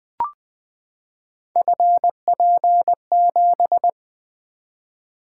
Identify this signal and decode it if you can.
Morse code, heard as a keyed tone, about 20 wpm, 700 Hz. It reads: FP7
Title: Three Letter-Number Combinations - 20wpm